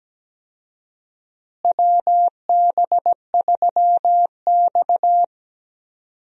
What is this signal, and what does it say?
Morse code, heard as a keyed tone, about 17 wpm, 700 Hz. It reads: WB3X
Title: USA Call Signs - 4 Characters - Repeated 3x - 17wpm